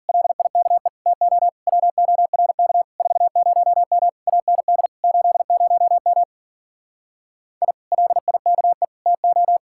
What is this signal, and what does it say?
Morse code, heard as a keyed tone, about 49 wpm, 700 Hz. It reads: LIKE TO WORK 40M AND 80M I LIKE TO
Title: Speed-Racing Common QSO Phrases Encore - 40wpm